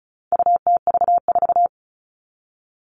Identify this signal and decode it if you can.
Morse code, heard as a keyed tone, about 34 wpm, 700 Hz. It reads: UTV4
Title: All Cut Numbers - 4-Digit 35wpm